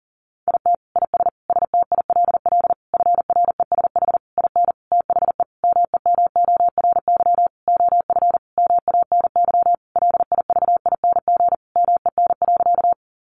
Morse code, heard as a keyed tone, about 40 wpm, 700 Hz. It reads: IT IS STILL FRESH IN THE MEMORY OF MANY LIVING MEN.
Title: Sentences from Top 1300 Words 40wpm